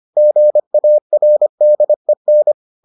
Morse code, heard as a keyed tone, about 25 wpm, 600 Hz. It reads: GARDEN